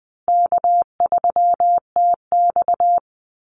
Morse code, heard as a keyed tone, about 20 wpm, 700 Hz. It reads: K3TX
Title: Easy Call Signs Repeated 3x - 20wpm